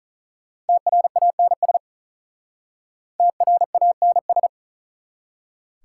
Morse code, heard as a keyed tone, about 47 wpm, 690 Hz. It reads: TRANS TRANS